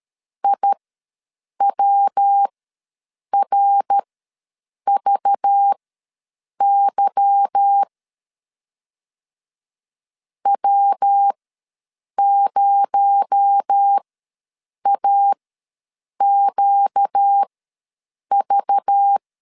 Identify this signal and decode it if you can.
Morse code, heard as a keyed tone, about 13 wpm, spaced as for 7 wpm, 800 Hz. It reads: IWRVY W0AQV
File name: LEZIONI / 8 YJ45 / 8 YJ ws 12wpm 3342